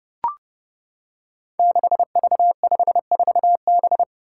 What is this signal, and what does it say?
Morse code, heard as a keyed tone, about 28 wpm, 700 Hz. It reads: 6V54B
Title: Five Challenging Letter-Number Combinations 30wpm